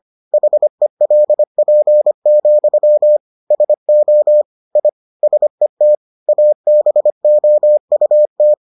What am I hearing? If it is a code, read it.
Morse code, heard as a keyed tone, about 25 wpm, 600 Hz. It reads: HELP, SO I SET ABOUT